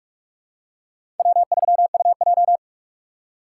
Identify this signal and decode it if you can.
Morse code, heard as a keyed tone, about 44 wpm, 700 Hz. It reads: W3UJ
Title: USA Call Signs - 4 Characters - 3x Character Spacing - 45wpm